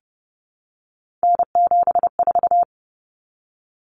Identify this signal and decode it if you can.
Morse code, heard as a keyed tone, about 30 wpm, 700 Hz. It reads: N74
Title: Common Cut Numbers - 3-Digit 30wpm